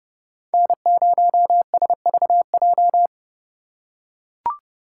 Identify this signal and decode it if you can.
Morse code, heard as a keyed tone, about 30 wpm, 700 Hz. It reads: N0SVJ